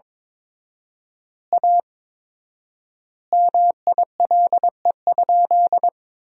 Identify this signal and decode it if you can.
Morse code, heard as a keyed tone, about 22 wpm, 700 Hz. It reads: A MILE?
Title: 4x Word-Spacing - Sentences from Top 400 Words - 22wpm